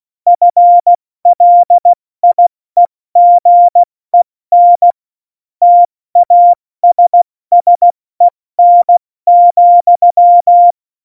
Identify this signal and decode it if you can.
Morse code, heard as a keyed tone, about 16 wpm, spaced as for 14 wpm, 700 Hz. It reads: FLIEGEN TASSEN,
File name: Morsekurs / Witze / Witz 17.Dezember 2019, 70 F80